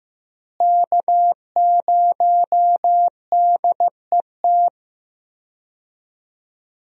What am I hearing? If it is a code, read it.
Morse code, heard as a keyed tone, about 15 wpm, 700 Hz. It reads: K0DET
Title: USA Call Signs - 5 Characters - Repeated 3x - 15wpm